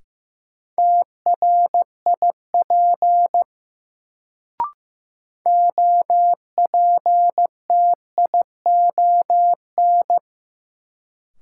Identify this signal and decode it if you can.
Morse code, heard as a keyed tone, about 15 wpm, 700 Hz. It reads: TRIP OPTION
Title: Sentences from Top 1300 Words - New Word Drill Alt - 15wpm